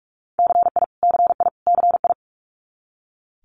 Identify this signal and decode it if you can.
Morse code, heard as a keyed tone, about 46 wpm, 700 Hz. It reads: CI CI CI
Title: Multi-letter Phonemes Repeated 3x 45wpm